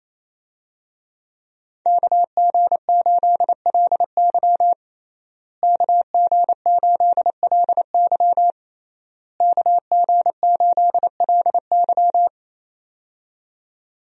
Morse code, heard as a keyed tone, about 27 wpm, 700 Hz. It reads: KG8LY KG8LY KG8LY
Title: USA Call Signs - 5 Characters - Repeated 3x - 28wpm